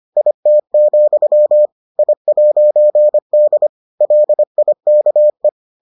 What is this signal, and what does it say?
Morse code, heard as a keyed tone, about 25 wpm, 600 Hz. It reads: IT, I'D LIKE